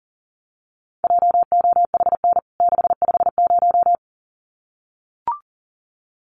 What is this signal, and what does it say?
Morse code, heard as a keyed tone, about 40 wpm, 700 Hz. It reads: JOHN 650
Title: CWT Exchange 40wpm